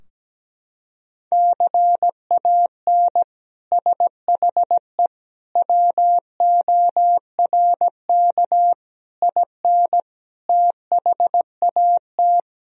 Morse code, heard as a keyed tone, about 17 wpm, 700 Hz. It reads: CAN SHE WORK IN THAT
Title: Sentences from Top 1000 Words - Review Part 4 - 17wpm